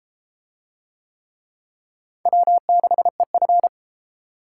Morse code, heard as a keyed tone, about 31 wpm, 700 Hz. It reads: W6EF